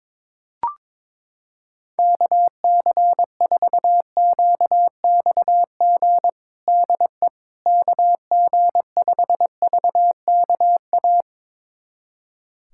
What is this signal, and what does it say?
Morse code, heard as a keyed tone, about 22 wpm, 700 Hz. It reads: KC4QXG DE KG5VKA
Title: USA Callsign DE Callsign - 6 Characters - 22wpm